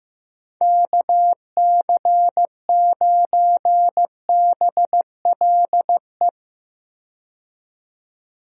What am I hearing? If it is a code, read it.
Morse code, heard as a keyed tone, about 15 wpm, 700 Hz. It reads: KC9BLE
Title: USA Call Signs - 6 Characters - 15wpm